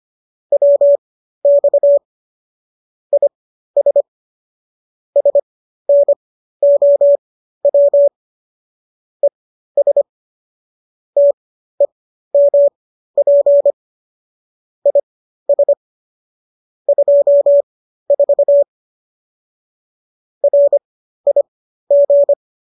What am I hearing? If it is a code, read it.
Morse code, heard as a keyed tone, about 25 wpm, spaced as for 13 wpm, 570 Hz. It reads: WX IS SNOW ES TEMP IS 24 RIG